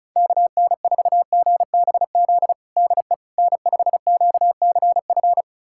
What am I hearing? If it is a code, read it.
Morse code, heard as a keyed tone, about 36 wpm, 700 Hz. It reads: KN4GBZ DE N5QCF
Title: USA Callsign DE Callsign 35wpm